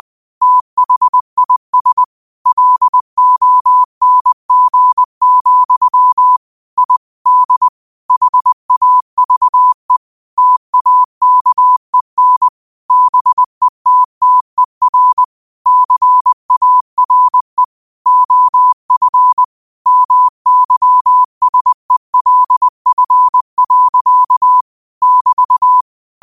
Morse code, heard as a keyed tone, about 20 wpm, 1000 Hz. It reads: THIS LONG, I D HAVE TAKEN BETTER CARE OF MYSELF. <BT>